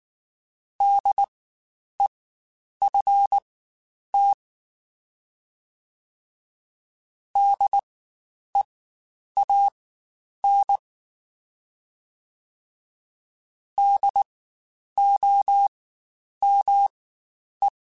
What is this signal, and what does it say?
Morse code, heard as a keyed tone, about 19 wpm, spaced as for 9 wpm, 800 Hz. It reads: DEFT DEAN DOME
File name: Spaced Alphas / Spaced Alphas mp3s / Ds sp @18